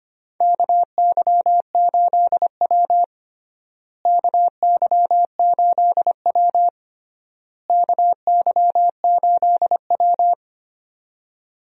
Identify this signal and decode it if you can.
Morse code, heard as a keyed tone, about 25 wpm, 700 Hz. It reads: KY8W KY8W KY8W
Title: USA Call Signs - 4 Characters - Repeated 3x - 25wpm